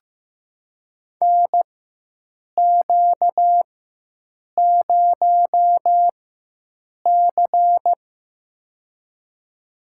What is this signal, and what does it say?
Morse code, heard as a keyed tone, about 15 wpm, spaced as for 7 wpm, 700 Hz. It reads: NQ0C
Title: USA Call Signs - 4 Characters - 4x Character Spacing - 15wpm